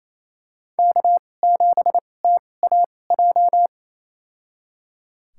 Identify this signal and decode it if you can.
Morse code, heard as a keyed tone, about 28 wpm, spaced as for 20 wpm, 700 Hz. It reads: K7TAJ
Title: USA Call Signs - 5 Characters - 2x Character Spacing - 28wpm